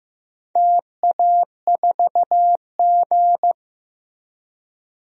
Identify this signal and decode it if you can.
Morse code, heard as a keyed tone, about 16 wpm, 700 Hz. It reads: TA4G
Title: DXCC-100 Call Signs - 4 Characters - Repeated 2x - 15wpm